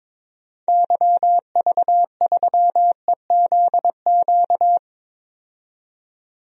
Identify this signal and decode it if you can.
Morse code, heard as a keyed tone, about 22 wpm, 700 Hz. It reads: YV3EZQ
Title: DXCC-100 Call Signs - 6 Characters - 22wpm